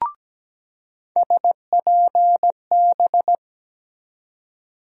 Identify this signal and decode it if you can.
Morse code, heard as a keyed tone, about 17 wpm, 700 Hz. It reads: SPB
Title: USA Call Sign Postfixes - 3 Letters 17wpm